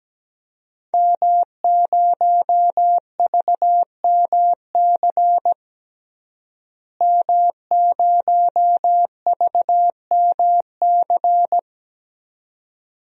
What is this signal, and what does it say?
Morse code, heard as a keyed tone, about 17 wpm, 700 Hz. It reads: M0VMC M0VMC
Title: DXCC-100 Call Signs - 5 Characters - Repeated 2x - 17wpm